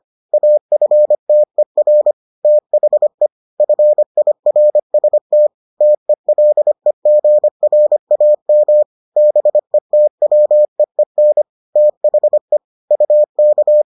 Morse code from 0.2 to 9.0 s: AFTER THE FIRST TELEGRAM

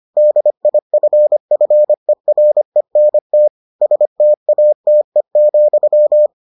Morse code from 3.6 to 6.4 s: STATE,

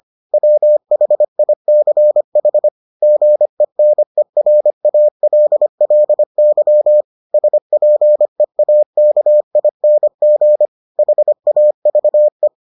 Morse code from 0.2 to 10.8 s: WHICH GENERALLY SPEAKING

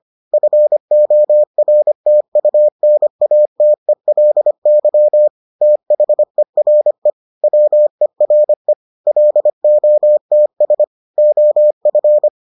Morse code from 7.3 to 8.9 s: WERE

